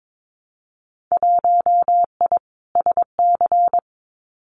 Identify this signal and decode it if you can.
Morse code, heard as a keyed tone, about 22 wpm, 700 Hz. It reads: 1I SC
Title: Winter Field Day Exchanges Encore 22wpm